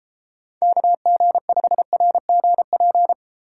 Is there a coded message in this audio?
Yes: Morse code from 0.6 to 3.1 s, heard as a keyed tone, about 32 wpm, 700 Hz. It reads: KG5RGP